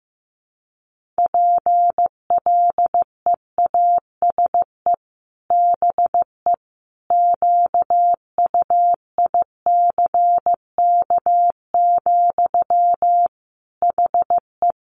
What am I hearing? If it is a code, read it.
Morse code, heard as a keyed tone, about 15 wpm, 700 Hz. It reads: PLEASE BE QUICK, HE
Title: Sentences from Top 1500 Words 15wpm